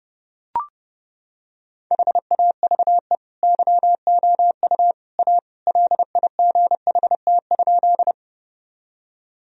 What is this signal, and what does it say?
Morse code, heard as a keyed tone, about 32 wpm, 700 Hz. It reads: HAVE YOU A LIGHT?
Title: Sentences from Top 1000 Words - Review Part 1 - 30wpm